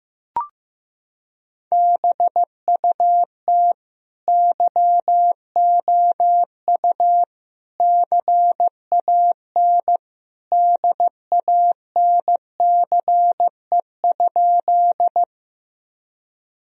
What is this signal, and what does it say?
Morse code, heard as a keyed tone, about 15 wpm, 700 Hz. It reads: BUT YOU CAN DANCE?
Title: Sentences from Top 1000 Words - Review Part 4 - 15wpm